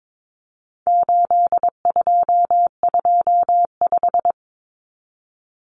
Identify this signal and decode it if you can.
Morse code, heard as a keyed tone, about 22 wpm, 700 Hz. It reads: 8225
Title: Numbers - 4-Digit 22wpm